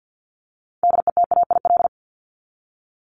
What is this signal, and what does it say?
Morse code, heard as a keyed tone, about 50 wpm, 700 Hz. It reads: DETAIL
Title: Top 1251-1500 Words 50wpm